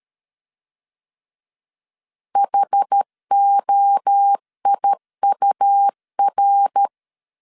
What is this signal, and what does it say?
Morse code, heard as a keyed tone, about 13 wpm, 800 Hz. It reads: HOIUR
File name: CW / DET / 1 3 1500C S C 12wpm 2750